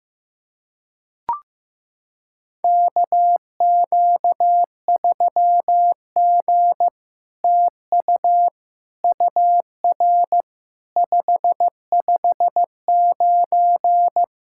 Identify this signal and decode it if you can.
Morse code, heard as a keyed tone, about 15 wpm, 700 Hz. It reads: KQ3G TU UR 559
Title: SOTA QSO Practice 15wpm